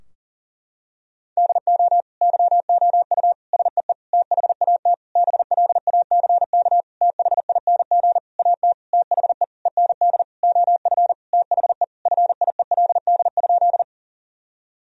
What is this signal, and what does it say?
Morse code, heard as a keyed tone, about 40 wpm, 700 Hz. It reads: DO YOU SEE THAT BLACK THING AT THE END OF THE FIELD?
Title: Sentences from Top 1000 Words - Review Part 2 - 40wpm